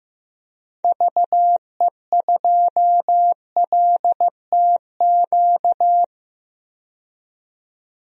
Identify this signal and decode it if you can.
Morse code, heard as a keyed tone, about 15 wpm, 700 Hz. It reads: VE2LTQ
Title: DXCC-100 Call Signs - 6 Characters - 15wpm